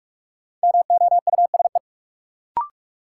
Morse code, heard as a keyed tone, about 46 wpm, 700 Hz. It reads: MOUSE
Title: Sentences from Top 700 Words - New Word Drill - 45wpm